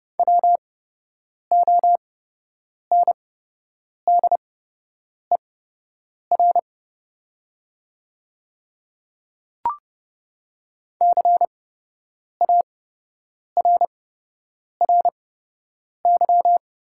Morse code, heard as a keyed tone, about 30 wpm, spaced as for 8 wpm, 700 Hz. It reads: WONDER CARRY
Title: 8x Character Spacing - Top 500 Words - 30wpm